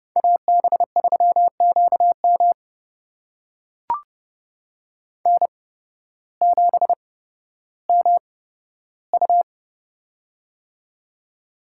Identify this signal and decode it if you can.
Morse code, heard as a keyed tone, about 30 wpm, spaced as for 14 wpm, 700 Hz. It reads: AB3QM N7MU
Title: USA Call Signs - 8x Character Spacing - 30wpm